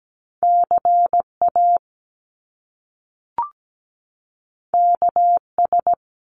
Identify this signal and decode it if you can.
Morse code, heard as a keyed tone, about 17 wpm, 700 Hz. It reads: CA KS
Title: US States 17wpm